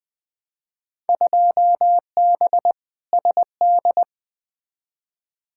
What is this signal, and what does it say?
Morse code, heard as a keyed tone, about 20 wpm, 700 Hz. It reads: 2B SD
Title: Field Day Exchanges Encore - 20wpm